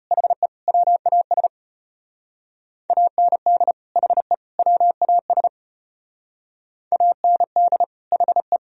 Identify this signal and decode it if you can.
Morse code, heard as a keyed tone, about 33 wpm, 700 Hz. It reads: HE WAS AND HE WAS AND HE